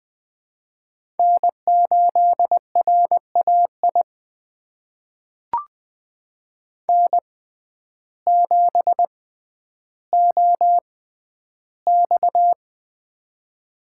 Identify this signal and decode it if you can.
Morse code, heard as a keyed tone, about 20 wpm, spaced as for 11 wpm, 700 Hz. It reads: N8RAI N7OX
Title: USA Call Signs - 6x Character Spacing - 20wpm